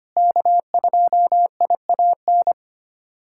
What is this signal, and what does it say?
Morse code, heard as a keyed tone, about 25 wpm, 700 Hz. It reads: K2IAN
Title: USA Callsign DE Callsign - 5 Characters - 25wpm